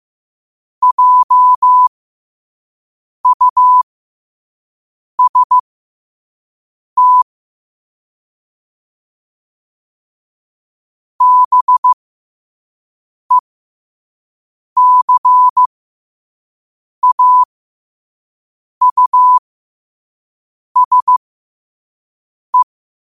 Morse code, heard as a keyed tone, about 15 wpm, spaced as for 5 wpm, 1000 Hz. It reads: JUST BECAUSE
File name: qotdfiles / 5wpm